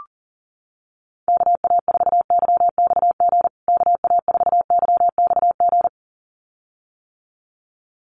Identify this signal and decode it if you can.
Morse code, heard as a keyed tone, about 40 wpm, 700 Hz. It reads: KA4YXG KA4YXG
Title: USA Call Signs Repeated 2x 40wpm